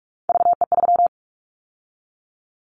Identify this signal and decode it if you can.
Morse code, heard as a keyed tone, about 46 wpm, 690 Hz. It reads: VE3
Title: Callsign Prefixes for Top 50 DXCC - 45wpm